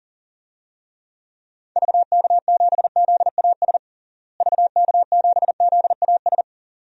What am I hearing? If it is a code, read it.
Morse code, heard as a keyed tone, about 40 wpm, 700 Hz. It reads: VK7ZAS VK7ZAS
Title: DXCC-100 Call Signs - 6 Characters - Repeated 2x - 40wpm